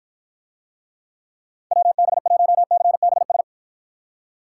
Morse code, heard as a keyed tone, about 53 wpm, 700 Hz. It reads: WB1CBS